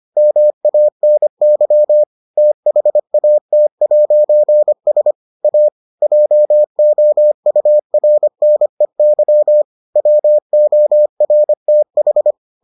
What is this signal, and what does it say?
Morse code, heard as a keyed tone, about 25 wpm, 600 Hz. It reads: MANY THAT'S A JOURNEY WORTH